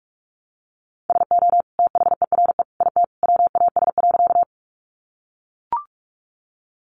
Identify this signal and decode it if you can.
Morse code, heard as a keyed tone, about 44 wpm, 700 Hz. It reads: SO THERE IT WAS.
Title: Sentences from Top 100 Words 45wpm